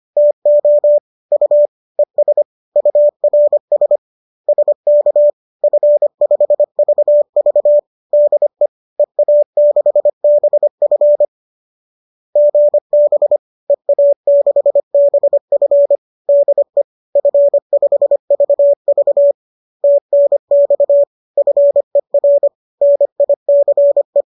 Morse code, heard as a keyed tone, about 25 wpm, 590 Hz. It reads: TO U ES URS SK F5VV DE EA6BF GB EA6BF DE F5VV TNX FER NICE